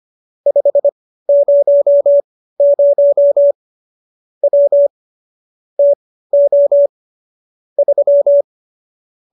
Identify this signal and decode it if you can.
Morse code, heard as a keyed tone, about 25 wpm, spaced as for 15 wpm, 570 Hz. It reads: 500 W TO 3